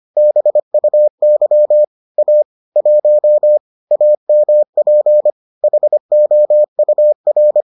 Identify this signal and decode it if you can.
Morse code, heard as a keyed tone, about 25 wpm, 600 Hz. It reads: BUY A 1 AMP HOUR